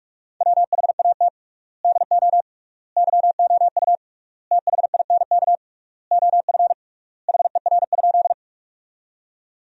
Morse code, heard as a keyed tone, about 46 wpm, 700 Hz. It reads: WHAT DO YOU THINK OF HER?
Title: 3x Word-Spacing - Sentences from Top 200 Words - 45wpm